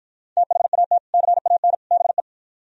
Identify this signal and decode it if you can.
Morse code, heard as a keyed tone, about 53 wpm, 700 Hz. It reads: THAT CAN BE